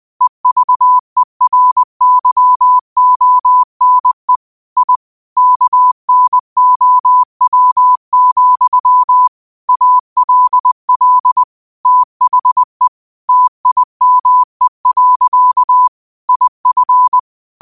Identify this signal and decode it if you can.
Morse code, heard as a keyed tone, about 20 wpm, 1000 Hz. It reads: EVERYONE I KNOW, ALL THE TIME. IF
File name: qotdfiles / 20wpm